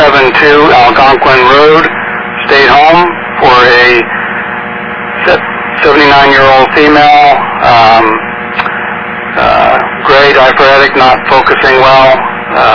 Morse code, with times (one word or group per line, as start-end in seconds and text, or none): none